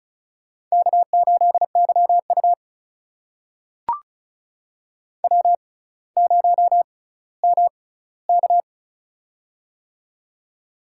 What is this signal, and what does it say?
Morse code, heard as a keyed tone, about 36 wpm, spaced as for 18 wpm, 700 Hz. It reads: K8YU W0MK
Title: USA Call Signs - 4 Characters - 6x Character Spacing - 35wpm